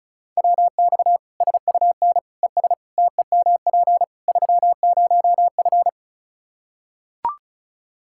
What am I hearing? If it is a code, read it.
Morse code, heard as a keyed tone, about 36 wpm, 700 Hz. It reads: WX SUN ES TEMP 30F
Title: Practice QSO - Exchange 2 - Info - 35wpm